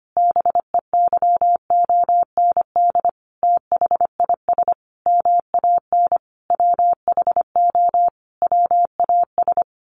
Morse code, heard as a keyed tone, about 25 wpm, 700 Hz. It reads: BEYOND THIS MAN WHO WAS
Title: A Sound Of Thunder 25wpm